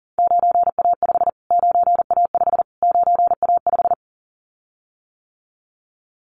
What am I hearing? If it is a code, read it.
Morse code, heard as a keyed tone, about 40 wpm, 700 Hz. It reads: 9A5 9A5 9A5
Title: Callsign Prefixes for Top 25 DXCC - Repeated 3x - 40wpm